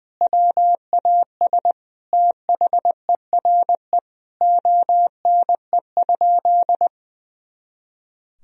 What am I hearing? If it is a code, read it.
Morse code, heard as a keyed tone, about 21 wpm, 700 Hz. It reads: WAS THERE ONE?
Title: Sentences from Top 100 Words and Abbreviations 20wpm